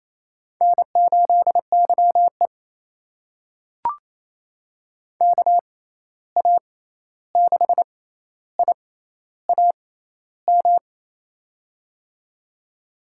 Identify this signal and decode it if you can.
Morse code, heard as a keyed tone, about 28 wpm, spaced as for 13 wpm, 700 Hz. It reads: N8YE KA6IAM
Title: USA Call Signs - 6x Character Spacing - 28wpm